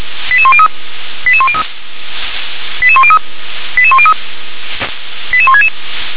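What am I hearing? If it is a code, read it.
Five-tone selective call (ZVEI), heard as quick runs of five tones, tones of about 70 ms. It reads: 88103 88103 88103 88103 88166
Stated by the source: ZVEI five-tone call received on SDR.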